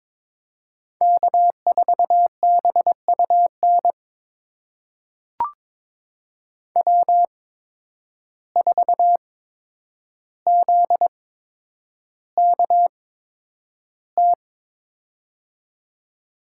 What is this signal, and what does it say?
Morse code, heard as a keyed tone, about 22 wpm, spaced as for 9 wpm, 700 Hz. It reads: K4BUN W4ZKT
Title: USA Call Signs - 5 Characters - 8x Character Spacing - 22wpm